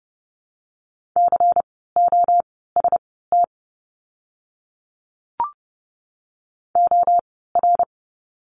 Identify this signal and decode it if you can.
Morse code, heard as a keyed tone, about 30 wpm, spaced as for 17 wpm, 700 Hz. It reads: COST OR
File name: Lesson 17f - Letter u - Words 30wpm